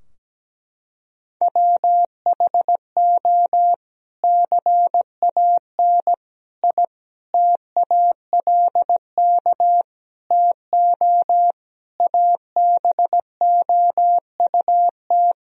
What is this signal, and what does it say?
Morse code, heard as a keyed tone, about 17 wpm, 700 Hz. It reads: WHO CAN I TALK TO ABOUT
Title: Sentences from Top 1000 Words - Review Part 4 - 17wpm